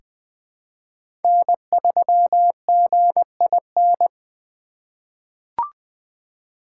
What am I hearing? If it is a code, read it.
Morse code, heard as a keyed tone, about 20 wpm, 700 Hz. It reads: N3GIN
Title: USA Call Signs - 5 Characters - 8x Character Spacing - 20wpm